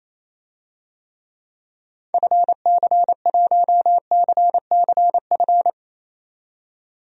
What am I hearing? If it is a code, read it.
Morse code, heard as a keyed tone, about 28 wpm, 700 Hz. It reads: FC1CCF
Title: DXCC-100 Call Signs - 6 Characters - 28wpm